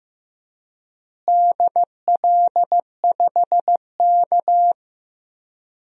Morse code, heard as a keyed tone, about 15 wpm, 700 Hz. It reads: DL5K
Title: DXCC-100 Call Signs - 4 Characters - Repeated 2x - 15wpm